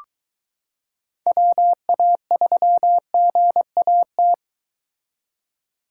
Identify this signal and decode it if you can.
Morse code, heard as a keyed tone, about 23 wpm, 700 Hz. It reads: WA3GAT